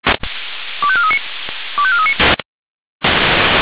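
Five-tone selective call (ZVEI), heard as quick runs of five tones, tones of about 70 ms. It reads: 25439 25439
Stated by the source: ZVEI five-tone selective call received on an SDR.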